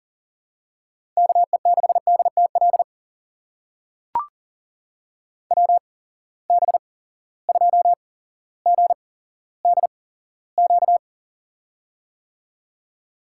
Morse code, heard as a keyed tone, about 40 wpm, spaced as for 17 wpm, 700 Hz. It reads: KE6DTL WB2GDQ
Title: USA Call Signs - 6 Characters - 8x Character Spacing - 40wpm